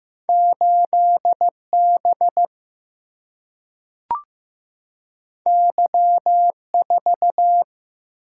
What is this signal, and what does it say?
Morse code, heard as a keyed tone, about 15 wpm, 700 Hz. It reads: 8 B Y 4
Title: Single-Letter Single-Number 15wpm